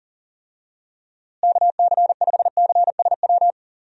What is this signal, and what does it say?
Morse code, heard as a keyed tone, about 40 wpm, 700 Hz. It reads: KC5CSW
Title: USA Call Signs - 6 Characters - Repeated 3x - 40wpm